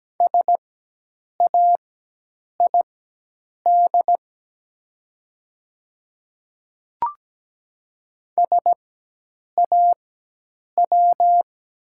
Morse code, heard as a keyed tone, about 17 wpm, spaced as for 8 wpm, 700 Hz. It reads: SAID SAW
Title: 4x Character Spacing - Top 300 Words - 17wpm